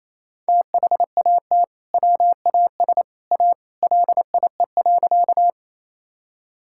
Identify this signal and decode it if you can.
Morse code, heard as a keyed tone, about 28 wpm, 700 Hz. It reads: THAT WAS A LIE.
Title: Sentences from Top 1600 Words 28wpm